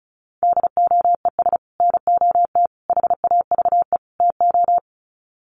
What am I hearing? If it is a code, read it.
Morse code, heard as a keyed tone, about 34 wpm, 700 Hz. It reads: DOES NOT HAVE TO
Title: Sets of 4 Words 35wpm